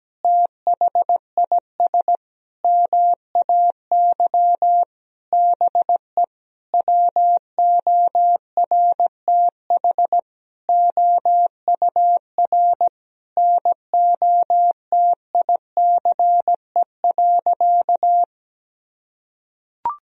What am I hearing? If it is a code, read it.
Morse code, heard as a keyed tone, about 17 wpm, 700 Hz. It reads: THIS MAY BE WORTH OUR NOTICE.
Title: Sentences from Top 1000 Words - Review Part 2 - 17wpm